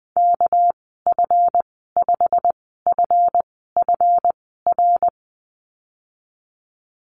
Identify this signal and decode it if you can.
Morse code, heard as a keyed tone, about 19 wpm, spaced as for 14 wpm, 700 Hz. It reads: KF5FFR
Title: 200 Lesson 32g - Letter k - Call Signs 20wpm